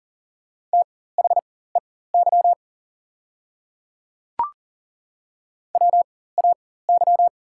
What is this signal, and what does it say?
Morse code, heard as a keyed tone, about 40 wpm, spaced as for 19 wpm, 700 Hz. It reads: THEY WAY
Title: 4x Character Spacing - Top 100 Words - 40wpm